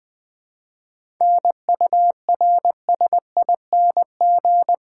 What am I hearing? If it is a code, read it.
Morse code, heard as a keyed tone, about 20 wpm, 700 Hz. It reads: NURSING
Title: 7 Letter Words - First 3 Letters Given - 20wpm